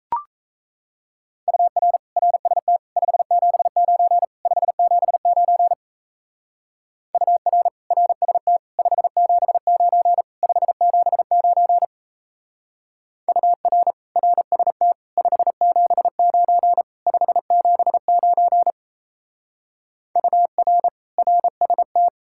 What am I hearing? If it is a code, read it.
Morse code, heard as a keyed tone, about 34 wpm, 700 Hz. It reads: UR RST 579 579 UR RST 579 579 UR RST 579 579 UR RST
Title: Speed-Racing Common QSO Phrases - 28wpm